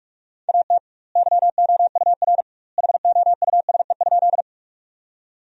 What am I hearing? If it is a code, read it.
Morse code, heard as a keyed tone, about 42 wpm, 700 Hz. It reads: AT YOUR HOUSE?
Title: 2x Word-Spacing - Sentences from Top 200 Words - 45wpm